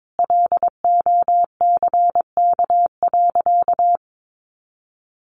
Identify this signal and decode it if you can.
Morse code, heard as a keyed tone, about 22 wpm, 700 Hz. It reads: LOCK.
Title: Sentences from Top 1300 Words 22wpm